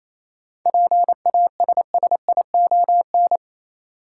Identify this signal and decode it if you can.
Morse code, heard as a keyed tone, about 28 wpm, 700 Hz. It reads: PASSION
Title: Sentences from Top 1300 Words - New Word Drill Alt - 28wpm